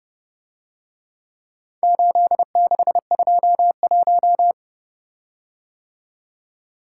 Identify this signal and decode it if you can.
Morse code, heard as a keyed tone, about 29 wpm, 700 Hz. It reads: 8621